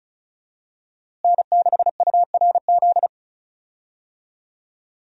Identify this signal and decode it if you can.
Morse code, heard as a keyed tone, about 36 wpm, 700 Hz. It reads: N6URZ
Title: USA Call Signs - 5 Characters - Repeated 2x - 35wpm